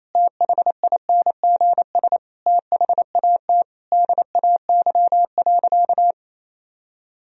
Morse code, heard as a keyed tone, about 28 wpm, 700 Hz. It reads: THINGS THAT DAY.
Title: Sentences from Top 1000 Words - Review Part 4 - 28wpm